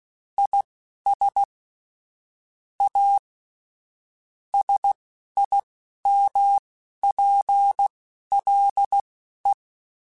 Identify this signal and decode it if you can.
Morse code, heard as a keyed tone, about 16 wpm, spaced as for 11 wpm, 800 Hz. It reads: IS A SIMPLE
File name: Technical Subjects / Technical Subjects-mp3s / Sw Pwr Sup @10